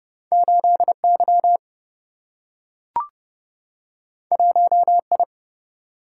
Morse code, heard as a keyed tone, about 30 wpm, 700 Hz. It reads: 8Y 1I
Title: Single-Letter Single-Number 30wpm